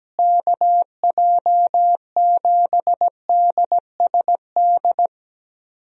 Morse code, heard as a keyed tone, about 17 wpm, 700 Hz. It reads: KJ7DSD
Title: USA Call Signs - 6 Characters - Repeated 3x - 17wpm